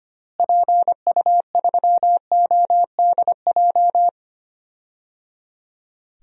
Morse code, heard as a keyed tone, about 25 wpm, 700 Hz. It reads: PU3ODJ
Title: DXCC-100 Call Signs - 6 Characters - Repeated 3x - 25wpm